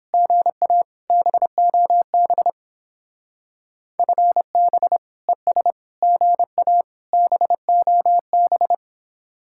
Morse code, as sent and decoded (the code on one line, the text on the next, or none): --. .- / -... --- -... / ..-. -... / . ... / --. .- / -... --- -...
GA BOB FB ES GA BOB